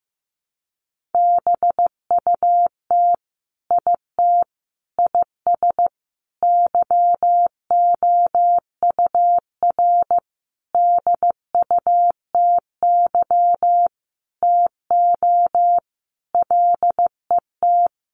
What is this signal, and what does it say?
Morse code, heard as a keyed tone, about 15 wpm, 700 Hz. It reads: BUT IT IS YOUR DUTY TO LET
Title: Sentences from Top 1500 Words 15wpm